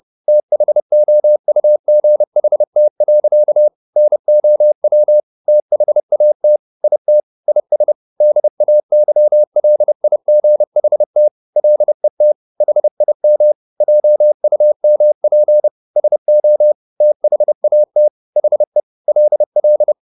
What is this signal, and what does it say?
Morse code, heard as a keyed tone, about 30 wpm, 600 Hz. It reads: THOUGHT. NOW THAT IT IS DAYLIGHT LET HIM JUMP SO THAT HE LL
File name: the old man and the sea / the old man and the sea 1 30wpm